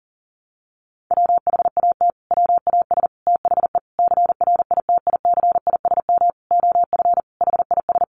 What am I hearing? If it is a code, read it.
Morse code, heard as a keyed tone, about 40 wpm, 700 Hz. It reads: WHAT WAS THE CRITICISM OF HIS
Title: Sentences from Top 1300 Words 40wpm